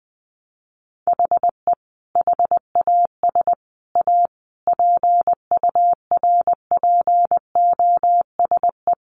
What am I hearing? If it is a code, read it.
Morse code, heard as a keyed tone, about 20 wpm, 700 Hz. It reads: HE HAS A PURPOSE
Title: Sentences from Top 800 Words 20wpm